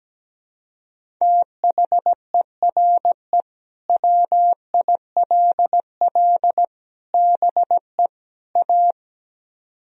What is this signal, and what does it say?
Morse code, heard as a keyed tone, about 17 wpm, 700 Hz. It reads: THERE WILL BE A
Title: Speed-Racing - Sets of 4 Words 17wpm